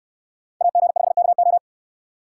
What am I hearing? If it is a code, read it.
Morse code, heard as a keyed tone, about 69 wpm, 700 Hz. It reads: AG5KO